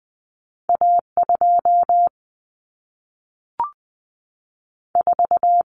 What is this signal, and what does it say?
Morse code, heard as a keyed tone, about 20 wpm, spaced as for 3 wpm, 700 Hz. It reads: A 24
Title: Common Cut Numbers - 2-Digit 20wpm